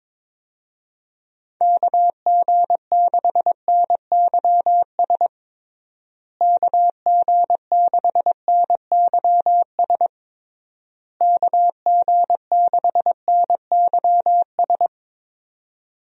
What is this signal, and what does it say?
Morse code, heard as a keyed tone, about 22 wpm, 700 Hz. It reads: KG6NYS KG6NYS KG6NYS
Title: USA Call Signs - 6 Characters - Repeated 3x - 22wpm